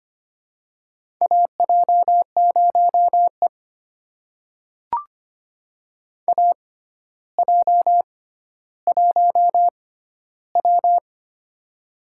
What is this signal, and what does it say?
Morse code, heard as a keyed tone, about 25 wpm, spaced as for 13 wpm, 700 Hz. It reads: AJ0E AJ1W
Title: USA Call Signs - 4 Characters - 6x Character Spacing - 25wpm